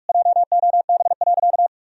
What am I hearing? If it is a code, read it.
Morse code, heard as a keyed tone, about 46 wpm, 700 Hz. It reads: JOB.
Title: Sentences from Top 1000 Words - Review Part 3 - 45wpm